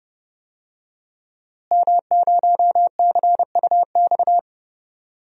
Morse code, heard as a keyed tone, about 30 wpm, 700 Hz. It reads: M0CUX
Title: DXCC-100 Call Signs - 5 Characters - Repeated 3x - 30wpm